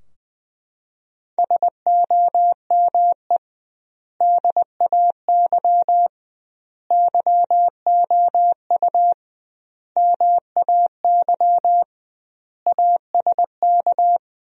Morse code, heard as a keyed tone, about 21 wpm, 700 Hz. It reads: SOME DAY YOU MAY ASK
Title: 2x Word-Spacing - Sentences from Top 200 Words - 20wpm